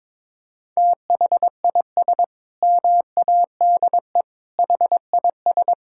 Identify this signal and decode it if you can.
Morse code, heard as a keyed tone, about 22 wpm, 700 Hz. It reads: THIS MADE HIS
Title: Sentences from Top 1600 Words 22wpm